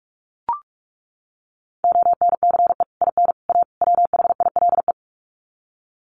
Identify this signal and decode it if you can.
Morse code, heard as a keyed tone, about 44 wpm, 700 Hz. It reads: ONCE IN A WHILE
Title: Sets of 4 Words 45wpm